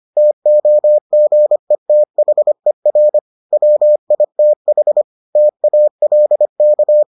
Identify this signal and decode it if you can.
Morse code, heard as a keyed tone, about 25 wpm, 600 Hz. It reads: TOGETHER WITH TALK